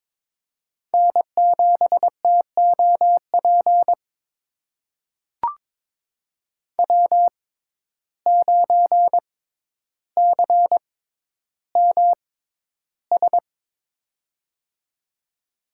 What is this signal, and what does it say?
Morse code, heard as a keyed tone, about 22 wpm, spaced as for 11 wpm, 700 Hz. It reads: N7TOP W9CMS
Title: USA Call Signs - 5 Characters - 6x Character Spacing - 22wpm